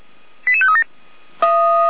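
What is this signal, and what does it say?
Five-tone selective call (ZVEI), heard as quick runs of five tones, tones of about 70 ms. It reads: 88537
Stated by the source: ZVEI five-tone selective call received by SDR.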